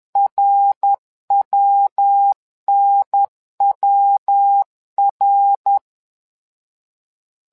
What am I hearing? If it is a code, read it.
Morse code, heard as a keyed tone, about 10 wpm, 800 Hz. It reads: RWNWR
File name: LEZIONI / 3 MWRO / 3 MWRO 10wpm 2148